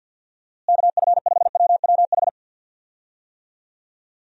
Morse code, heard as a keyed tone, about 51 wpm, 700 Hz. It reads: KF5WWH
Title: USA Call Signs - 6 Characters - Repeated 2x - 50wpm